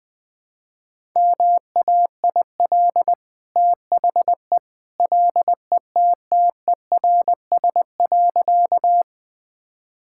Morse code, heard as a keyed tone, about 21 wpm, 700 Hz. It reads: MAIL THE LETTERS.
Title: Easy Sentences with Hints - 20wpm